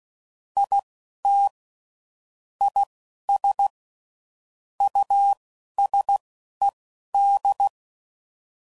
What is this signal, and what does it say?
Morse code, heard as a keyed tone, about 16 wpm, spaced as for 11 wpm, 800 Hz. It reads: IT IS USED
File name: CW Code Groups @10WPM / Technical Subjects / Reactance